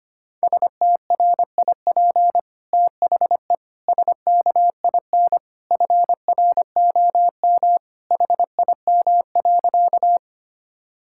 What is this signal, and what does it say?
Morse code, heard as a keyed tone, about 25 wpm, 700 Hz. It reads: STRIP THE SKIN FROM HIM.
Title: Sentences from Top 1600 Words 25wpm